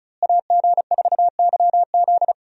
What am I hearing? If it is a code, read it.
Morse code, heard as a keyed tone, about 36 wpm, 700 Hz. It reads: AG4YZ
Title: USA Call Signs - 5 Characters - Repeated 2x - 35wpm